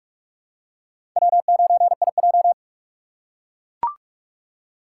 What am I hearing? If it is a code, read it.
Morse code, heard as a keyed tone, about 41 wpm, 700 Hz. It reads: W9IJ
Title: USA Call Signs - 2x Character Spacing - 45wpm